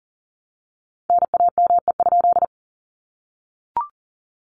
Morse code, heard as a keyed tone, about 40 wpm, 700 Hz. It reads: NAME?
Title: Common QSO Phrases Encore 40wpm